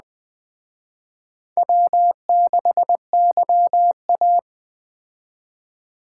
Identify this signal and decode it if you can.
Morse code, heard as a keyed tone, about 20 wpm, 700 Hz. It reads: W6YA